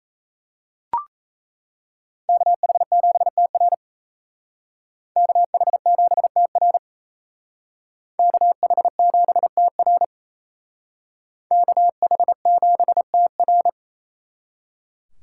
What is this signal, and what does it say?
Morse code, heard as a keyed tone, about 32 wpm, 700 Hz. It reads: KH7TR KH7TR KH7TR KH7TR
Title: Speed Racing - USA 2x2 Call Signs 28wpm